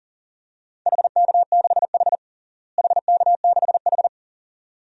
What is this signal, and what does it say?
Morse code, heard as a keyed tone, about 40 wpm, 700 Hz. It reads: HK6H HK6H
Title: DXCC-100 Call Signs - 4 Characters - Repeated 2x - 40wpm